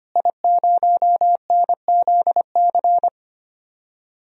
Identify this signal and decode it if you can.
Morse code, heard as a keyed tone, about 25 wpm, 700 Hz. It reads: I0NZC
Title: DXCC-100 Call Signs - 5 Characters - Repeated 3x - 25wpm